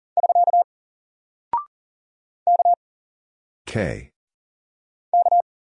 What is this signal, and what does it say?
Morse code, heard as a keyed tone, about 40 wpm, 700 Hz. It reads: <SK> K K
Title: Single Character (Letter:Number:Punctuation:Prosign) Alt 40wpm